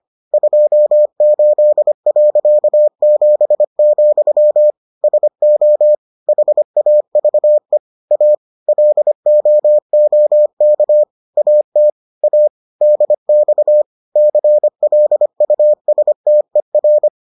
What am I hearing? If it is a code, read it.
Morse code, heard as a keyed tone, about 25 wpm, 600 Hz. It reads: 28.7, SO HAVE A LOOK AT A DX CLUSTER